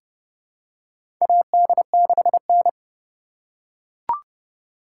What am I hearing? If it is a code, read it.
Morse code, heard as a keyed tone, about 32 wpm, 700 Hz. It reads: AD6N
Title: USA Call Signs - 4 Characters - 3x Character Spacing - 30wpm